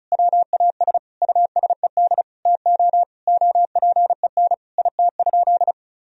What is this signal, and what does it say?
Morse code, heard as a keyed tone, about 34 wpm, 700 Hz. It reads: WAS USED TO OPEN IT?
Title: Sentences from Top 1000 Words - Review Part 4 - 35wpm